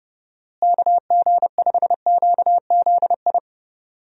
Morse code, heard as a keyed tone, about 32 wpm, 700 Hz. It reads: KG5QZI
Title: USA Call Signs - 6 Characters - Repeated 3x - 30wpm